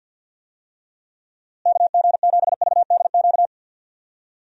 Morse code, heard as a keyed tone, about 46 wpm, 700 Hz. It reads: KG7VDQ
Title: USA Call Signs - 6 Characters - Repeated 2x - 50wpm